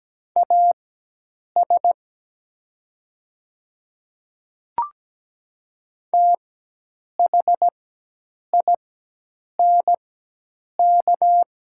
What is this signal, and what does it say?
Morse code, heard as a keyed tone, about 17 wpm, spaced as for 8 wpm, 700 Hz. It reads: AS THINK
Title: 4x Character Spacing - Top 300 Words - 17wpm